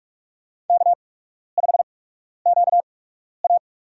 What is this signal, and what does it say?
Morse code, heard as a keyed tone, about 42 wpm, spaced as for 12 wpm, 700 Hz. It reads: K5QA
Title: USA Call Signs - 4 Characters - 8x Character Spacing - 45wpm